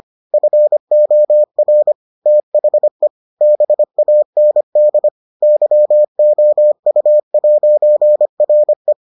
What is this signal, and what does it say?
Morse code, heard as a keyed tone, about 25 wpm, 600 Hz. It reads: FOR THE BAND YOU'RE